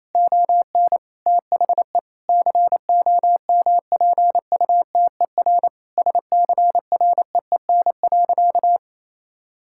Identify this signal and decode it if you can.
Morse code, heard as a keyed tone, about 28 wpm, 700 Hz. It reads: ON THE COMPUTER SCREEN.
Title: Sentences from Top 1000 Words - Review Part 6 - 28wpm